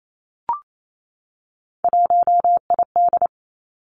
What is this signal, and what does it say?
Morse code, heard as a keyed tone, about 28 wpm, 700 Hz. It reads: 1ID
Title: Three Challenging Letter-Number Combinations 28wpm